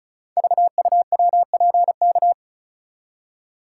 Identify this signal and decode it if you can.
Morse code, heard as a keyed tone, about 36 wpm, 690 Hz. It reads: VUWPK
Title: Code Groups - 5 Letters - 35wpm